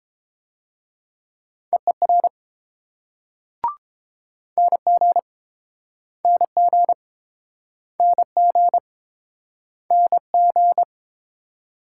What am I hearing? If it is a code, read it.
Morse code, heard as a keyed tone, about 30 wpm, 700 Hz. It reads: EER NG NG NG NG